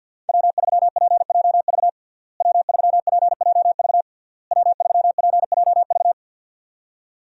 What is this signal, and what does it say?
Morse code, heard as a keyed tone, about 52 wpm, 700 Hz. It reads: W3PJV W3PJV W3PJV
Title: USA Call Signs - 5 Characters - Repeated 3x - 50wpm